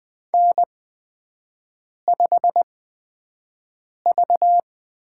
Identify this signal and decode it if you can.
Morse code, heard as a keyed tone, about 20 wpm, spaced as for 5 wpm, 700 Hz. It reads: N5V
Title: USA Call Signs - 4 Characters - 8x Character Spacing - 20wpm